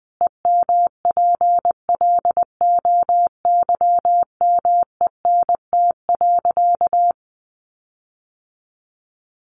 Morse code, heard as a keyed tone, about 19 wpm, 700 Hz. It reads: EMPLOYMENT.
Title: Sentences from Top 1500 Words 20wpm